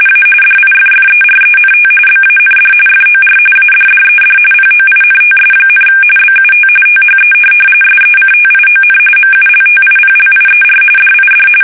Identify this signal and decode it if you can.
RTTY, about 45.45 baud. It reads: A DIFFERENT BAND FOR ADDED POINTS, BUT THE SECTION MULTIPLIER DID
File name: fsk850 / teletype-RTTY